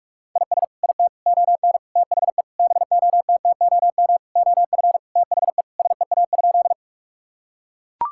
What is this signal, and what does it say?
Morse code, heard as a keyed tone, about 42 wpm, 700 Hz. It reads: IS IT ON THE BOTTOM OF THE SEA?
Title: Sentences from Top 1000 Words - Review Part 6 - 45wpm